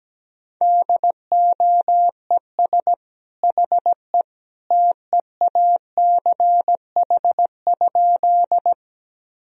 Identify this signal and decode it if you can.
Morse code, heard as a keyed tone, about 17 wpm, 700 Hz. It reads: DOES HE TEACH?
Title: Sentences from Top 1000 Words - Review Part 2 - 17wpm